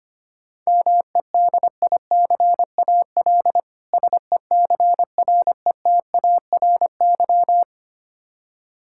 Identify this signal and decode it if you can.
Morse code, heard as a keyed tone, about 25 wpm, 700 Hz. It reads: MEDICAL SECRETARY
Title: Practice QSO - Exchange 3 - More Info - 25wpm